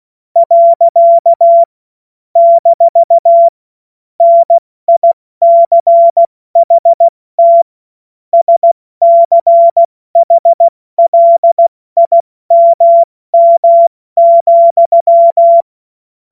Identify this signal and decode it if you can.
Morse code, heard as a keyed tone, about 16 wpm, 680 Hz. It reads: . - NICHT SCHLIMM,
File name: Morsekurs / Witze / Witz 03 Juni Mai 2025 70 F80